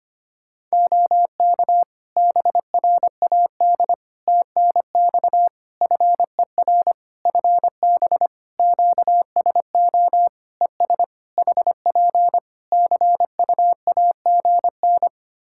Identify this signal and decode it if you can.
Morse code, heard as a keyed tone, about 25 wpm, 700 Hz. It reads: OK BRAD TNX FER FB QSO ES HP CUAGN
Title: K1USN SST QSO Practice 25wpm